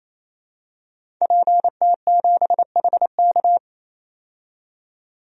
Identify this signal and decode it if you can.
Morse code, heard as a keyed tone, about 28 wpm, 700 Hz. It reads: PT7HK
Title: DXCC-100 Call Signs - 5 Characters - 28wpm